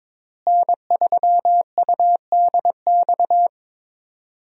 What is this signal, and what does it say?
Morse code, heard as a keyed tone, about 23 wpm, 700 Hz. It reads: N3UDX
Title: USA Call Signs - 5 Characters - Repeated 3x - 22wpm